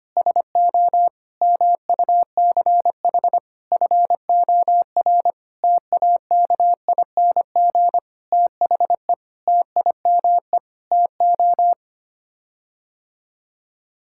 Morse code, heard as a keyed tone, about 25 wpm, 700 Hz. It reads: SO MUCH FOR TAKING THE TIME TO
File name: Sets of 9 Words - 25wpm